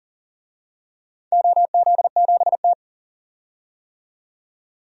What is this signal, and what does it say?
Morse code, heard as a keyed tone, about 38 wpm, 700 Hz. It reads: OZ7T
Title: DXCC-100 Call Signs - 4 Characters - 40wpm